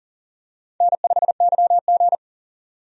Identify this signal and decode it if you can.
Morse code, heard as a keyed tone, about 40 wpm, 700 Hz. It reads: N5YG
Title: USA Call Signs - 4 Characters - 40wpm